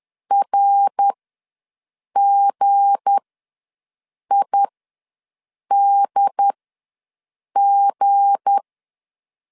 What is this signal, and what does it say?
Morse code, heard as a keyed tone, about 10 wpm, spaced as for 6 wpm, 800 Hz. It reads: RGIDG
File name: LEZIONI / 7 GF9Z / 7 G ws 10wpm 3843